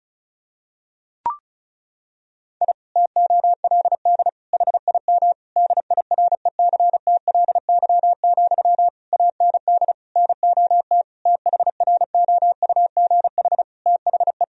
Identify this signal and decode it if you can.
Morse code, heard as a keyed tone, about 34 wpm, 700 Hz. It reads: I TOLD HIM DIRECTLY, AND NOT THROUGH THE
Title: Sentences from Top 1000 Words - Review Part 1 - 35wpm